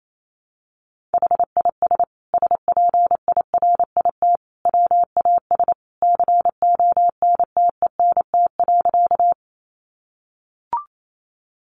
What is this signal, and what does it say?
Morse code, heard as a keyed tone, about 28 wpm, 700 Hz. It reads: HIS SPIRIT WAS CONTENT.
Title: Sentences from Top 1400 Words 28wpm